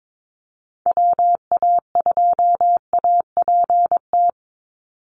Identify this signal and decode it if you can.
Morse code, heard as a keyed tone, about 22 wpm, 700 Hz. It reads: WA2APT
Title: USA Call Signs 22wpm